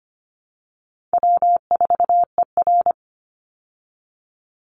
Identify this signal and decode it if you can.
Morse code, heard as a keyed tone, about 25 wpm, 700 Hz. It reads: W4ER
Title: Easy Call Signs Repeated 2x - 25wpm